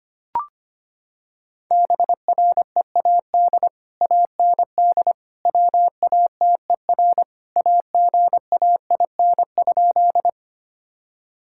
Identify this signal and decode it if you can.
Morse code, heard as a keyed tone, about 25 wpm, 700 Hz. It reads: BREAD AND WATER AGAIN?
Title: Sentences from Top 1600 Words 25wpm